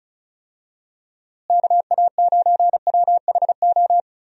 Morse code, heard as a keyed tone, about 34 wpm, 700 Hz. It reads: KA9WHO
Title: USA Call Signs - 6 Characters - Repeated 2x - 35wpm